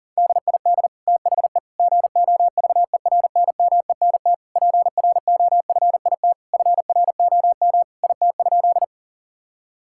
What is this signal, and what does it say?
Morse code, heard as a keyed tone, about 40 wpm, 700 Hz. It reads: DID THE GOVERNMENT PROFIT FROM IT?
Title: Sentences from Top 1000 Words - Review Part 5 - 40wpm